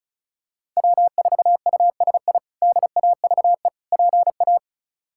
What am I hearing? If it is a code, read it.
Morse code, heard as a keyed tone, about 34 wpm, 700 Hz. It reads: W4USI DAVE PA
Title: K1USN SST QSO Practice - 35wpm